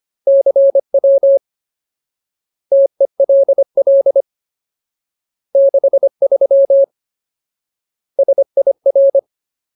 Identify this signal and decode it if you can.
Morse code, heard as a keyed tone, about 25 wpm, 550 Hz. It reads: CW TELL 63 SIR
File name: Ultimate Word List / 25WPM Ultimate